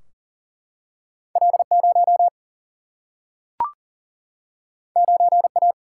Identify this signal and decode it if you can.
Morse code, heard as a keyed tone, about 40 wpm, 700 Hz. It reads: L 0 9 A
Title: Single-Letter Single-Number 40wpm